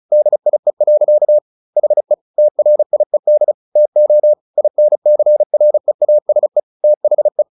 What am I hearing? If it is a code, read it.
Morse code, heard as a keyed tone, about 34 wpm, 600 Hz. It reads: DIE. HE TRIED TO INCREASE THE
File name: the old man and the sea / the old man and the sea 1 35wpm